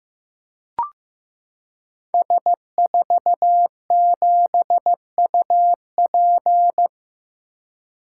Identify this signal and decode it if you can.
Morse code, heard as a keyed tone, about 15 wpm, 700 Hz. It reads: S47UP
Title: Code Groups - 5 Letter:Numbers - 15wpm